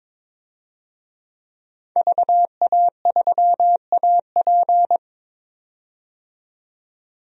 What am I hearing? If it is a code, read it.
Morse code, heard as a keyed tone, about 22 wpm, 700 Hz. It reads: VA3AP
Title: DXCC-100 Call Signs - 5 Characters - 22wpm